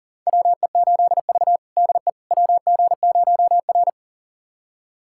Morse code, heard as a keyed tone, about 40 wpm, 700 Hz. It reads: WE8V DE WG0R
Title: USA Callsign DE Callsign - 4 Characters - 40wpm